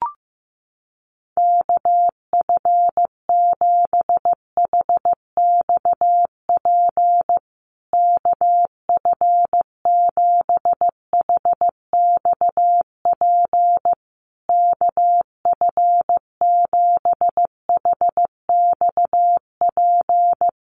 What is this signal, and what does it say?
Morse code, heard as a keyed tone, about 15 wpm, 700 Hz. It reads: KF7HXP KF7HXP KF7HXP
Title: USA Call Signs Repeated 3x 15wpm